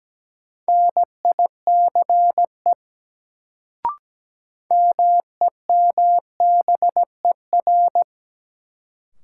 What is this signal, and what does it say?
Morse code, heard as a keyed tone, about 17 wpm, 700 Hz. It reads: NICE MEMBER
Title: Sentences from Top 1100 Words - New Word Drill Alt - 17wpm